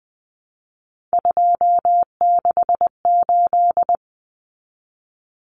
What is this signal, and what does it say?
Morse code, heard as a keyed tone, about 20 wpm, 700 Hz. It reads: 268
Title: Common Cut Numbers - 3-Digit 20wpm